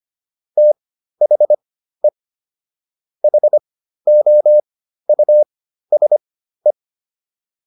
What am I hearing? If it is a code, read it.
Morse code, heard as a keyed tone, about 25 wpm, spaced as for 13 wpm, 600 Hz. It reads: THE HOUSE